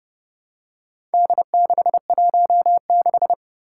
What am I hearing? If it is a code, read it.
Morse code, heard as a keyed tone, about 29 wpm, 700 Hz. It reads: D616
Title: Four Challenging Letter-Number Combinations 30wpm